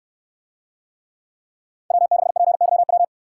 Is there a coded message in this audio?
Yes: Morse code from 1.9 to 3.0 s, heard as a keyed tone, about 64 wpm, 700 Hz. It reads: W6PYW